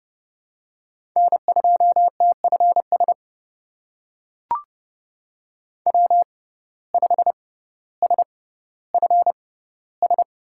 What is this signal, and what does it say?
Morse code, heard as a keyed tone, about 30 wpm, spaced as for 15 wpm, 700 Hz. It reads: N2TFS W5SFS
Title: USA Call Signs - 5 Characters - 6x Character Spacing - 30wpm